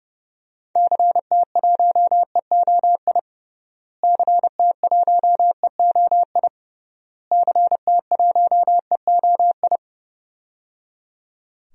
Morse code, heard as a keyed tone, about 30 wpm, 700 Hz. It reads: CT1EOI CT1EOI CT1EOI
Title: DXCC-100 Call Signs - 6 Characters - Repeated 3x - 30wpm